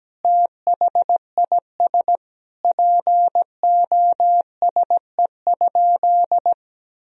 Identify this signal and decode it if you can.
Morse code, heard as a keyed tone, about 17 wpm, 700 Hz. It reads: THIS POSE?
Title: Sentences from Top 1000 Words - Review Part 2 - 17wpm